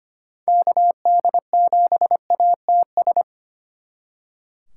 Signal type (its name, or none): Morse code